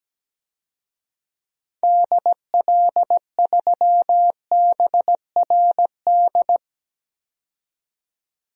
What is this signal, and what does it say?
Morse code, heard as a keyed tone, about 17 wpm, 700 Hz. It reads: DL3BRD
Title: DXCC-100 Call Signs - 6 Characters - 17wpm